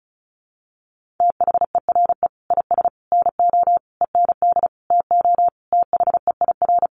Morse code, heard as a keyed tone, about 34 wpm, 700 Hz. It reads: THERE IS NO END TO THEIR
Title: Sentences from Top 1100 Words 35wpm